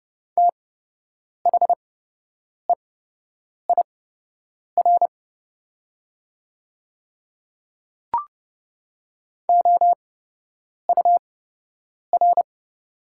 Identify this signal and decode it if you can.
Morse code, heard as a keyed tone, about 30 wpm, spaced as for 8 wpm, 700 Hz. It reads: THEIR OUR
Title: 8x Character Spacing - Top 300 Words - 30wpm